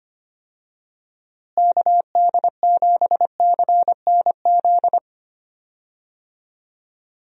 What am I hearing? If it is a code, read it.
Morse code, heard as a keyed tone, about 25 wpm, 700 Hz. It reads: KD7CNZ
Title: USA Call Signs - 6 Characters - Repeated 2x - 25wpm